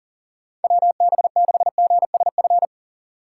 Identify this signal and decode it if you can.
Morse code, heard as a keyed tone, about 50 wpm, 700 Hz. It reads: WB6GSF